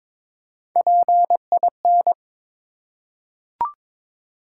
Morse code, heard as a keyed tone, about 23 wpm, 700 Hz. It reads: PIN
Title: Three Letter Words - 22wpm